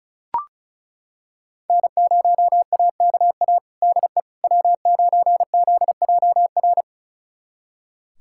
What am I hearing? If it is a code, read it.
Morse code, heard as a keyed tone, about 36 wpm, 700 Hz. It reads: N0AKA DE W9ZJR
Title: USA Callsign DE Callsign - 5 Characters - 35wpm